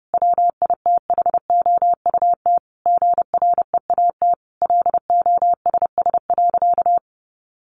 Morse code, heard as a keyed tone, about 30 wpm, 700 Hz. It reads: WITHOUT GREAT LOSS.
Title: Sentences from Top 1400 Words 30wpm